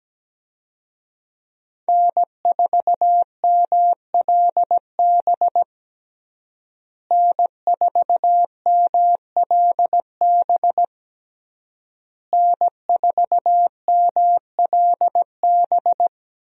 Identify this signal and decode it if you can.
Morse code, heard as a keyed tone, about 17 wpm, 700 Hz. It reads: N4MLB N4MLB N4MLB
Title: USA Call Signs - 5 Characters - Repeated 3x - 17wpm